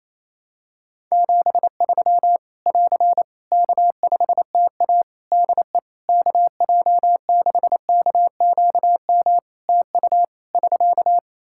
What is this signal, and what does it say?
Morse code, heard as a keyed tone, about 28 wpm, 700 Hz. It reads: 73 <AR> K5TA DE KJ6KQM TU <SK>
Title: Practice QSO - Exchange 4 - Ending - 28wpm